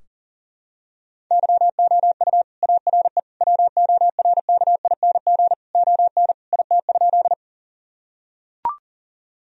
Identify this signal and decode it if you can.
Morse code, heard as a keyed tone, about 40 wpm, 700 Hz. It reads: YOU ARE WORKING ON IT?
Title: Sentences from Top 1000 Words - Review Part 4 - 40wpm